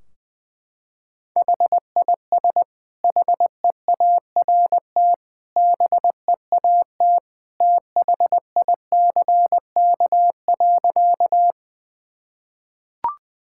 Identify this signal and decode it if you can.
Morse code, heard as a keyed tone, about 20 wpm, 700 Hz. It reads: HIS HEART BEAT THICK.
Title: Sentences from Top 1600 Words 20wpm